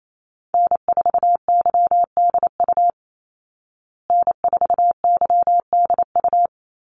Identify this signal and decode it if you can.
Morse code, heard as a keyed tone, about 28 wpm, 700 Hz. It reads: N4YDU N4YDU
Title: Call Signs Repeated 3x - 28wpm